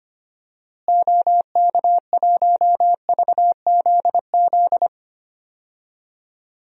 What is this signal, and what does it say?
Morse code, heard as a keyed tone, about 25 wpm, 700 Hz. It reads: OK1VZZ
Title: DXCC-100 Call Signs - 6 Characters - Repeated 2x - 25wpm